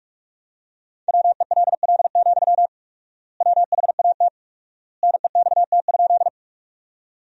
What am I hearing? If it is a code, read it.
Morse code, heard as a keyed tone, about 41 wpm, 700 Hz. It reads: WELL, WHAT NEXT?
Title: 4x Word-Spacing - Sentences from Top 300 Words - 45wpm